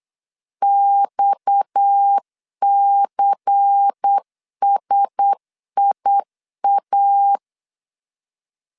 Morse code, heard as a keyed tone, about 8 wpm, 800 Hz. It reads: XCSIA
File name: LEZIONI / 6 PKX0 / 6 PKX 8wpm 3041